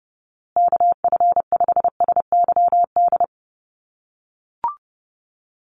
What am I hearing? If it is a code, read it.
Morse code, heard as a keyed tone, about 30 wpm, 700 Hz. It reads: KF5SYD
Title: USA Call Signs Repeated 2x 30wpm